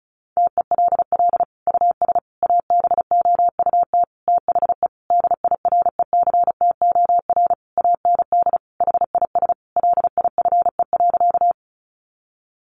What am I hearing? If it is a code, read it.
Morse code, heard as a keyed tone, about 34 wpm, 700 Hz. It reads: TELL US ABOUT THE DIRECTOR AND HIS LIFE.
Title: Sentences from Top 1300 Words 35wpm